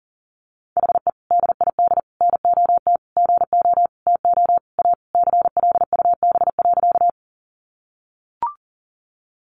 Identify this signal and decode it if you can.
Morse code, heard as a keyed tone, about 40 wpm, 700 Hz. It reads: HE DID NOT GO TO A CLUB.
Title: Sentences from Top 1200 Words 40wpm